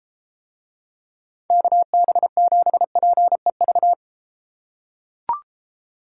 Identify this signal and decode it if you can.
Morse code, heard as a keyed tone, about 32 wpm, 700 Hz. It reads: KB7PEV